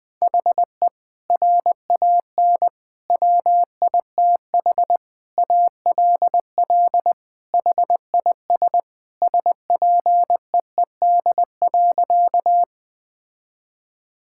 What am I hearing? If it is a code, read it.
Morse code, heard as a keyed tone, about 20 wpm, 700 Hz. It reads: HE RAN WITH ALL HIS SPEED.
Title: Sentences from Top 1000 Words - Review Part 5 - 20wpm